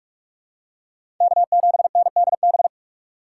Morse code, heard as a keyed tone, about 48 wpm, 700 Hz. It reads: K7NDB
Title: USA Call Signs - 5 Characters - 6x Character Spacing - 45wpm